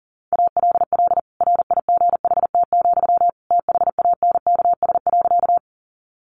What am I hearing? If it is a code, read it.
Morse code, heard as a keyed tone, about 40 wpm, 700 Hz. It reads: ALL RIGHT, THANKS.
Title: Sentences from Top 600 Words 40wpm